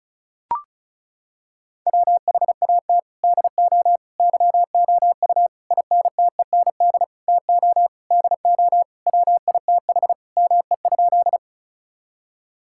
Morse code, heard as a keyed tone, about 36 wpm, 700 Hz. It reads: WHAT DO YOU INTEND TO DO WITH ME?
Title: Sentences from Top 1600 Words 35wpm